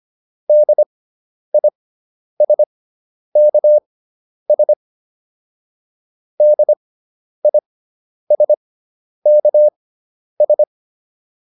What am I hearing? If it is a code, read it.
Morse code, heard as a keyed tone, about 25 wpm, spaced as for 10 wpm, 600 Hz. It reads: DISKS DISKS